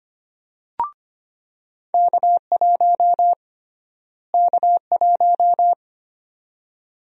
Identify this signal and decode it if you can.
Morse code, heard as a keyed tone, about 25 wpm, 700 Hz. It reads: K1 K1
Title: USA Call Sign Prefixes - Repeated 2x - 25wpm